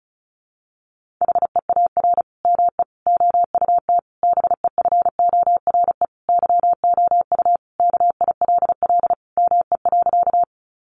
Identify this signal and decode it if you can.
Morse code, heard as a keyed tone, about 34 wpm, 700 Hz. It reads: HEAR ME OUT BEFORE YOU KILL ME.
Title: Sentences from Top 1400 Words 35wpm